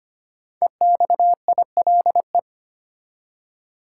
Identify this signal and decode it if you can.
Morse code, heard as a keyed tone, about 25 wpm, 700 Hz. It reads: EXILE
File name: 5 Letter Words - First 3 Letters Given - 25wpm